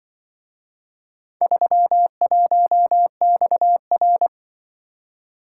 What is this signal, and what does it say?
Morse code, heard as a keyed tone, about 24 wpm, 700 Hz. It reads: 31XR
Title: Speed-Racing - Four Letter-Number Combinations - 20wpm